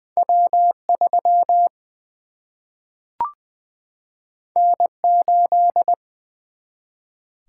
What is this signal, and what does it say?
Morse code, heard as a keyed tone, about 20 wpm, 700 Hz. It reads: W 3 N 8
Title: Single-Letter Single-Number 20wpm